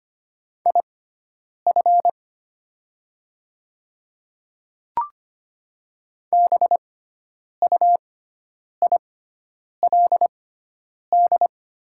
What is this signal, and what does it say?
Morse code, heard as a keyed tone, about 25 wpm, spaced as for 9 wpm, 700 Hz. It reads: IF BUILD
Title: 6x Character Spacing - Top 200 Words - 25wpm